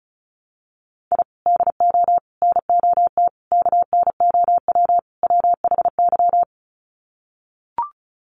Morse code, heard as a keyed tone, about 34 wpm, 700 Hz. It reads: I DO NOT KNOW WHY
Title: Sets of 5 Words 35wpm